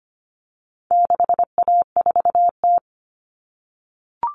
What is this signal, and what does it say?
Morse code, heard as a keyed tone, about 25 wpm, 700 Hz. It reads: 6A4T
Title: All Cut Numbers - 4-Digit 25wpm